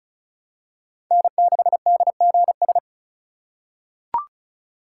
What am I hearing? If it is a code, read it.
Morse code, heard as a keyed tone, about 36 wpm, 700 Hz. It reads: N6DGS
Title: USA Call Signs - 5 Characters - 4x Character Spacing - 35wpm